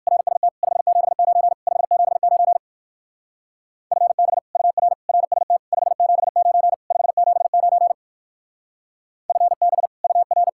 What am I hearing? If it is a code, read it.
Morse code, heard as a keyed tone, about 59 wpm, 700 Hz. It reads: RST 579 579 FB UR RST 579 579 FB UR